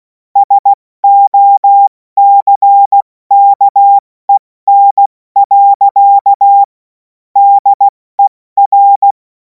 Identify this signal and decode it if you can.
Morse code, heard as a keyed tone, about 16 wpm, 800 Hz. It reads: SOCKEN. DER
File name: Morsekurs / Witze / Witz 16. Juni 2026 70 F80